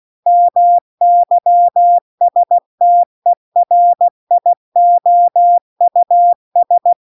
Morse code, heard as a keyed tone, about 16 wpm, 700 Hz. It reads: MYSTERIOUS